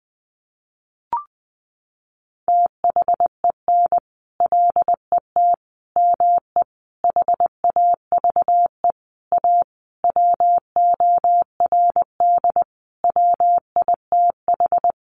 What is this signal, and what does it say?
Morse code, heard as a keyed tone, about 20 wpm, 700 Hz. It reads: THEN LET ME HAVE A WORD WITH
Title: Sentences from Top 300 Words 20wpm